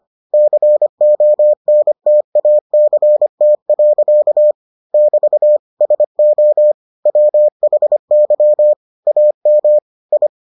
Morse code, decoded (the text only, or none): CONTACT. <BT> SO WHY AM I